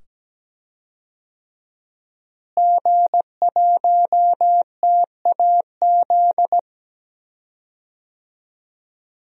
Morse code, heard as a keyed tone, about 17 wpm, 700 Hz. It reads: G1TAZ
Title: DXCC-100 Call Signs - 5 Characters - Repeated 2x - 17wpm